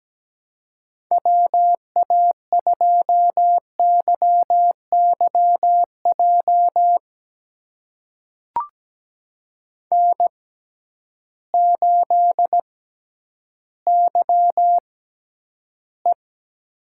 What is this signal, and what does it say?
Morse code, heard as a keyed tone, about 17 wpm, spaced as for 10 wpm, 700 Hz. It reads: WA2YYJ N8YE
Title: USA Call Signs - 6x Character Spacing - 17wpm